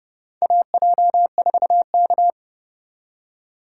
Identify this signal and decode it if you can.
Morse code, heard as a keyed tone, about 30 wpm, 700 Hz. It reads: AJ4K
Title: USA Call Signs - 4 Characters - 2x Character Spacing - 30wpm